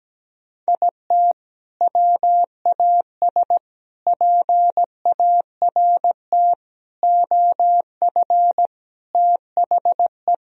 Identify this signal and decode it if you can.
Morse code, heard as a keyed tone, about 17 wpm, 700 Hz. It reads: IT WAS PART OF THE
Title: Sentences from Top 1000 Words - Review Part 4 - 17wpm